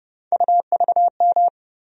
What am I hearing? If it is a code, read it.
Morse code, heard as a keyed tone, about 30 wpm, 700 Hz. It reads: UVM
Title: USA Call Sign Postfixes - 3 Letters 30wpm